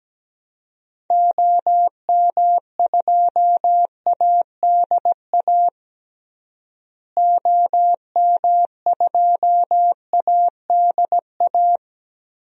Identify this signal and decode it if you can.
Morse code, heard as a keyed tone, about 17 wpm, 700 Hz. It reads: OM2ADA OM2ADA
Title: DXCC-100 Call Signs - 6 Characters - Repeated 3x - 17wpm